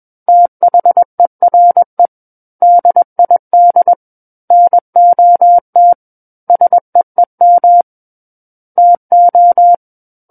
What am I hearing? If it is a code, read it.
Morse code, heard as a keyed tone, about 21 wpm, 700 Hz. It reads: THERE DID NOT SEEM TO
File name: Book One / 12-Chapter 12 - 21 WPM